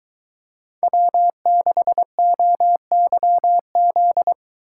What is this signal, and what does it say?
Morse code, heard as a keyed tone, about 22 wpm, 700 Hz. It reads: W6OYZ